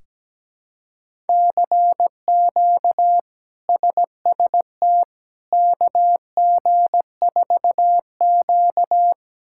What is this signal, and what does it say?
Morse code, heard as a keyed tone, about 17 wpm, 700 Hz. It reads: CQ SST KG4Q
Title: K1USN SST QSO Practice - 17wpm